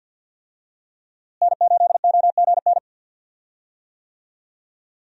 Morse code, heard as a keyed tone, about 53 wpm, 700 Hz. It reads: N8OGN
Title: USA Call Signs - 5 Characters - Repeated 2x - 50wpm